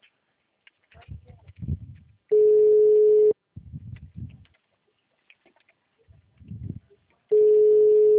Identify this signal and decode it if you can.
Ringback tone (Europe, 425 Hz).